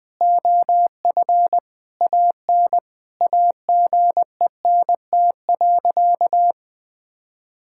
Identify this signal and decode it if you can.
Morse code, heard as a keyed tone, about 20 wpm, 700 Hz. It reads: OF AN AGENT.
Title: Sentences from Top 1600 Words 20wpm